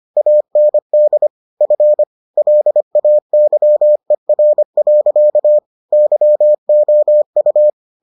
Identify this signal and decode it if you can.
Morse code, heard as a keyed tone, about 25 wpm, 600 Hz. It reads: AND F LAYER. YOU